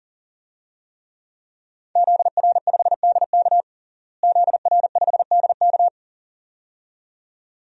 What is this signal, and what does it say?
Morse code, heard as a keyed tone, about 40 wpm, 700 Hz. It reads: ZR5DK ZR5DK
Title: DXCC-100 Call Signs - 5 Characters - Repeated 2x - 40wpm